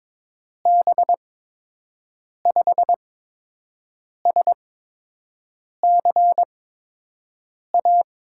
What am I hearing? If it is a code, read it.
Morse code, heard as a keyed tone, about 22 wpm, spaced as for 6 wpm, 700 Hz. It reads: B5SCA
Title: USA Call Signs - 6 Characters - 8x Character Spacing - 22wpm